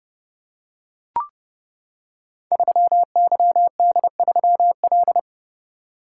Morse code, heard as a keyed tone, about 30 wpm, 700 Hz. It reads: 3YD3L
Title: Code Groups - 5 Letter:Numbers - 30wpm